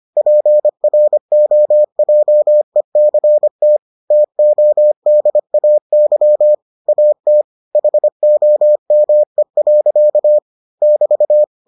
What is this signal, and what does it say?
Morse code, heard as a keyed tone, about 25 wpm, 600 Hz. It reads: PROJECT TODAY AT HOME. <BT>